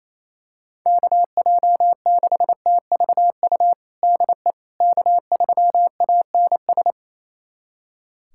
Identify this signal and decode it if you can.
Morse code, heard as a keyed tone, about 28 wpm, 700 Hz. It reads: KJ6TVU DE K3ANS
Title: USA Callsign DE Callsign 28wpm